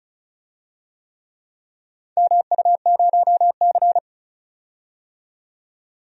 Morse code, heard as a keyed tone, about 36 wpm, 700 Hz. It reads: MU0C
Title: DXCC-100 Call Signs - 4 Characters - 35wpm